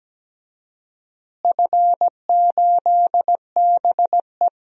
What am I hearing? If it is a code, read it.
Morse code, heard as a keyed tone, about 17 wpm, 700 Hz. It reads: F8BE
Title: DXCC-100 Call Signs - 4 Characters - Repeated 3x - 17wpm